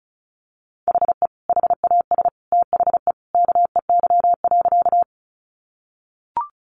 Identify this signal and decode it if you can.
Morse code, heard as a keyed tone, about 34 wpm, 700 Hz. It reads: HE HAS THE KEY.
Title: Sentences from Top 600 Words 35wpm